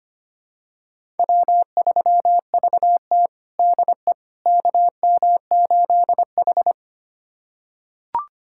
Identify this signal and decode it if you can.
Morse code, heard as a keyed tone, about 25 wpm, 700 Hz. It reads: W3VT DE KM8H
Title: USA Callsign DE Callsign - 4 Characters - 25wpm